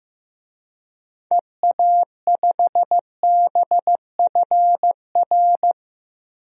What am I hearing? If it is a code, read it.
Morse code, heard as a keyed tone, about 15 wpm, 700 Hz. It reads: EA5BFR
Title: DXCC-100 Call Signs - 6 Characters - Repeated 3x - 15wpm